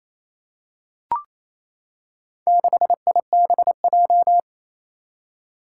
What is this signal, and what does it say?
Morse code, heard as a keyed tone, about 28 wpm, 700 Hz. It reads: 6IBJ
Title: Four Challenging Letter-Number Combinations 28wpm